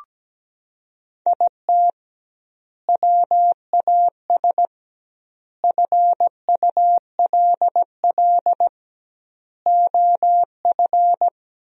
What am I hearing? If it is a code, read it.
Morse code, heard as a keyed tone, about 17 wpm, 700 Hz. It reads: IT WAS FULL OF
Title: 2x Word-Spacing - Sentences from Top 500 Words - 17wpm